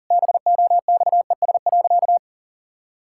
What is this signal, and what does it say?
Morse code, heard as a keyed tone, about 40 wpm, 700 Hz. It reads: BOXES.
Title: Pangram Challenge v2 - 40wpm